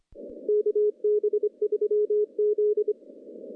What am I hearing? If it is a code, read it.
Morse code, heard as a keyed tone, about 25 wpm, 420 Hz. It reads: KB3Z
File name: contests / CW-SS-k5tr- -0100